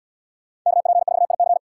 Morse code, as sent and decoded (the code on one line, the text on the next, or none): -.- .--. ....- .. -.-.
KP4IC